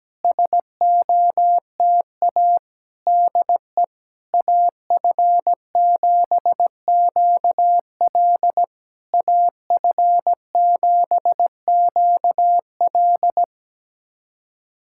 Morse code, heard as a keyed tone, about 17 wpm, 700 Hz. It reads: SOTA DE AF7QL AF7QL
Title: SOTA QSO Practice 17wpm